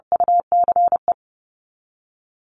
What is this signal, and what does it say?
Morse code, heard as a keyed tone, about 30 wpm, 700 Hz. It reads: UCE
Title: Top 500 Words Repeated 30wpm